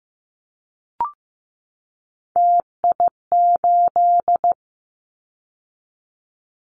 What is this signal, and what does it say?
Morse code, heard as a keyed tone, about 15 wpm, 700 Hz. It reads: TI8
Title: Callsign Prefixes for Top 100 DXCC - 15wpm